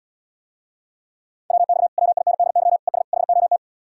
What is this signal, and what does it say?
Morse code, heard as a keyed tone, about 77 wpm, 700 Hz. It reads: WX WINDY ES HOT